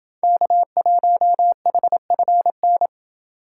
Morse code, heard as a keyed tone, about 28 wpm, 700 Hz. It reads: K1HFN